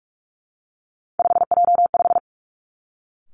Morse code, heard as a keyed tone, about 46 wpm, 700 Hz. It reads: 5J5
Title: Three Challenging Letter-Number Combinations 45wpm